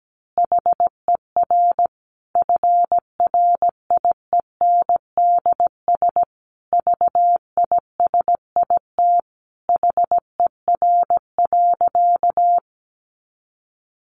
Morse code, heard as a keyed tone, about 17 wpm, 700 Hz. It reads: HER FRIENDS VISIT HER.
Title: Easy Phrases 17wpm